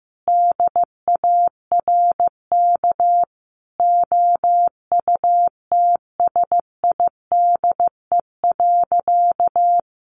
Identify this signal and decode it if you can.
Morse code, heard as a keyed tone, about 15 wpm, 700 Hz. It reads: DARK OUTSIDE.
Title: Sentences from Top 900 Words 15wpm